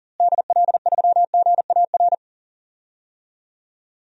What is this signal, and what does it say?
Morse code, heard as a keyed tone, about 40 wpm, 700 Hz. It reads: DL3GAR
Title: DXCC-100 Call Signs - 6 Characters - Repeated 3x - 40wpm